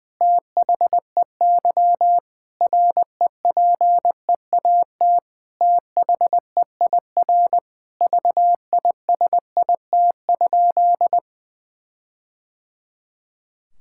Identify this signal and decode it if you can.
Morse code, heard as a keyed tone, about 20 wpm, 700 Hz. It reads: THEY REPEAT THEIR VISIT?
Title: Sentences from Top 1600 Words 20wpm